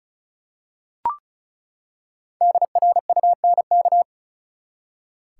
Morse code, heard as a keyed tone, about 36 wpm, 700 Hz. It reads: DRUNK
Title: Five Letter Words - 35wpm